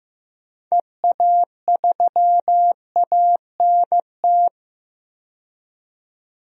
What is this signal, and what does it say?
Morse code, heard as a keyed tone, about 15 wpm, 700 Hz. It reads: EA3ANT
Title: DXCC-100 Call Signs - 6 Characters - 15wpm